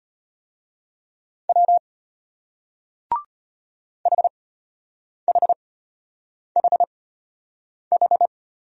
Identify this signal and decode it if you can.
Morse code, heard as a keyed tone, about 35 wpm, spaced as for 8 wpm, 700 Hz. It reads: W HHHH